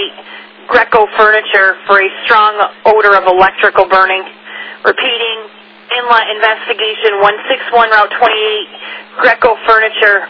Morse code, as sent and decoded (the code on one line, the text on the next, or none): none